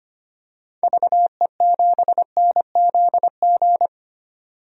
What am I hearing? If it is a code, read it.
Morse code, heard as a keyed tone, about 25 wpm, 700 Hz. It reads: VE7NZG
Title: DXCC-100 Call Signs - 6 Characters - 25wpm